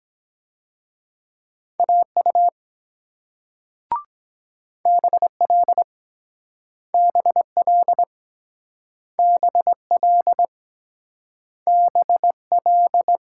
Morse code, heard as a keyed tone, about 23 wpm, 700 Hz. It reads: AU BL BL BL BL